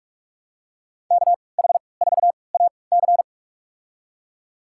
Morse code, heard as a keyed tone, about 46 wpm, spaced as for 25 wpm, 700 Hz. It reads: KH4AC
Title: USA Call Signs - 3x Character Spacing - 45wpm